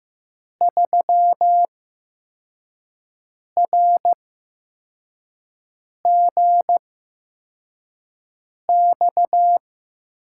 Morse code, heard as a keyed tone, about 15 wpm, spaced as for 4 wpm, 700 Hz. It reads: 3RGX
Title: USA Call Signs - 6 Characters - 8x Character Spacing - 15wpm